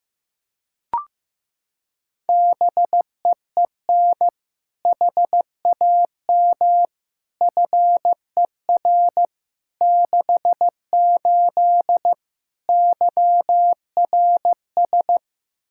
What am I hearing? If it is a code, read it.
Morse code, heard as a keyed tone, about 15 wpm, 700 Hz. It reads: BEEN HAM FER 68 YRS
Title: K1USN SST QSO Practice 15wpm